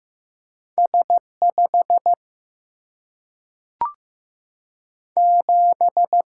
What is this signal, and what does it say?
Morse code, heard as a keyed tone, about 15 wpm, 700 Hz. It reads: S 5 7
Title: Single-Letter Single-Number 15wpm